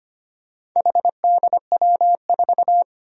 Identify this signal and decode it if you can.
Morse code, heard as a keyed tone, about 25 wpm, 700 Hz. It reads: HDW4
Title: Four Challenging Letter-Number Combinations 25wpm